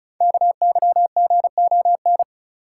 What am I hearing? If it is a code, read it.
Morse code, heard as a keyed tone, about 36 wpm, 690 Hz. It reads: KYGON
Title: Code Groups - 5 Letters - 35wpm